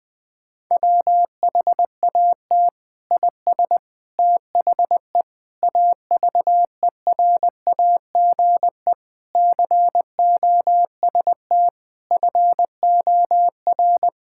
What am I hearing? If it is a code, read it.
Morse code, heard as a keyed tone, about 20 wpm, 700 Hz. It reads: WHAT IS THE AVERAGE COST FOR
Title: Sentences from Top 1000 Words - Review Part 6 - 20wpm